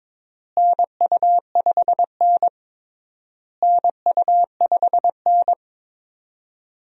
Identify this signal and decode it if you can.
Morse code, heard as a keyed tone, about 22 wpm, 700 Hz. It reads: NU5N NU5N
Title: USA Call Signs - 4 Characters - Repeated 3x - 22wpm